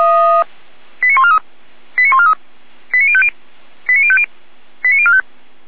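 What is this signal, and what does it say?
Five-tone selective call (ZVEI), heard as quick runs of five tones, tones of about 80 ms. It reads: 89143 89143 89050 89050 89045
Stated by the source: ZVEI five-tone call 89126 received by SDR, from the Feuerwehr Orlamünde fire brigade audio archive.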